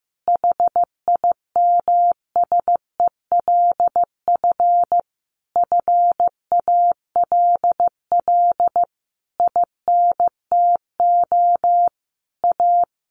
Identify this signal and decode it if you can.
Morse code, heard as a keyed tone, about 15 wpm, 700 Hz. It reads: HIMSELF FALL INTO A
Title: A Sound Of Thunder 15wpm